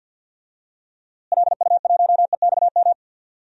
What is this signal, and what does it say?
Morse code, heard as a keyed tone, about 49 wpm, 700 Hz. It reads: LU1EXM
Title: DXCC-100 Call Signs - 6 Characters - Repeated 3x - 50wpm